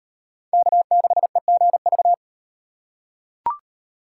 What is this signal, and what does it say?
Morse code, heard as a keyed tone, about 38 wpm, 700 Hz. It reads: K6EGV